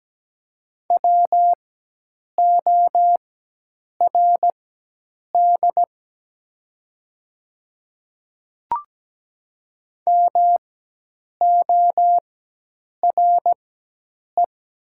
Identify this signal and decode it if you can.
Morse code, heard as a keyed tone, about 17 wpm, spaced as for 8 wpm, 700 Hz. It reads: WORD MORE
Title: 4x Character Spacing - Top 100 Words - 17wpm